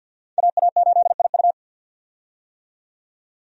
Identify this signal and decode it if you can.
Morse code, heard as a keyed tone, about 52 wpm, 700 Hz. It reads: AA8IU
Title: USA Call Signs - 5 Characters - 50wpm